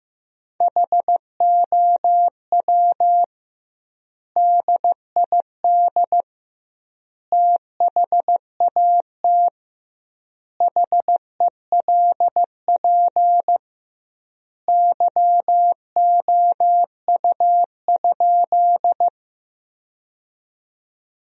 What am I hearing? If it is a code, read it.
Morse code, heard as a keyed tone, about 15 wpm, 700 Hz. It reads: HOW DID THAT HELP YOU?
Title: 2x Word-Spacing - Sentences from Top 200 Words - 15wpm